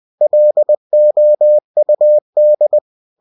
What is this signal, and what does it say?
Morse code, heard as a keyed tone, about 20 wpm, 600 Hz. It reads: LOUD
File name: cw-books / winnie-the-pooh / winnie-20wpm-page-0108